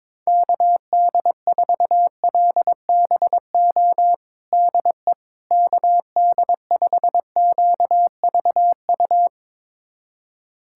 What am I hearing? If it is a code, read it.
Morse code, heard as a keyed tone, about 22 wpm, 700 Hz. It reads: KD4LBO DE KD5QVU
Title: USA Callsign DE Callsign - 6 Characters - 22wpm